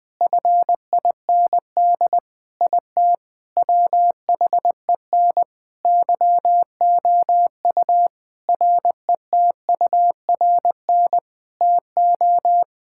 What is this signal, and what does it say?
Morse code, heard as a keyed tone, about 21 wpm, 700 Hz. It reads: FIND IT WHEN YOU RETURN TO
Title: Sentences from Top 1600 Words 20wpm